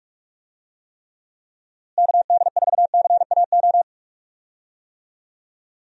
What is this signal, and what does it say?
Morse code, heard as a keyed tone, about 44 wpm, 700 Hz. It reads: KD4CAO
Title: USA Call Signs - 6 Characters - Repeated 2x - 45wpm